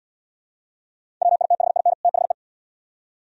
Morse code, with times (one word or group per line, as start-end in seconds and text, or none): 1.2-1.9 WITHIN
2.0-2.3 THE